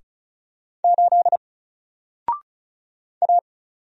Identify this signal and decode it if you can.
Morse code, heard as a keyed tone, about 36 wpm, 700 Hz. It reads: 8 A
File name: Single-Letter-Number-35wpm